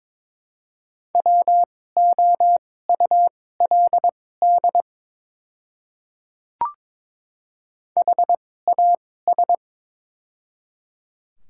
Morse code, heard as a keyed tone, about 22 wpm, spaced as for 16 wpm, 700 Hz. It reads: WOULD HAS
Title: 2x Character Spacing - Top 100 Words - 22wpm